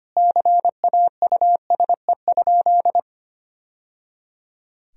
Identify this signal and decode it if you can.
Morse code, heard as a keyed tone, about 25 wpm, 700 Hz. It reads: CAUSE?
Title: Sentences from Top 1000 Words - Review Part 3 - 25wpm